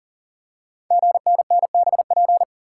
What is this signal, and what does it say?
Morse code, heard as a keyed tone, about 40 wpm, 700 Hz. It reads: GNNBP
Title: Code Groups - 5 Letters - 40wpm